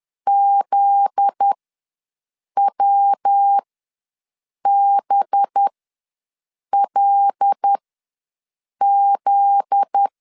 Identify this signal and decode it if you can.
Morse code, heard as a keyed tone, about 10 wpm, spaced as for 6 wpm, 800 Hz. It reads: ZWBLZ
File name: LEZIONI / 7 GF9Z / 7 GF9Z ws 10wpm 3919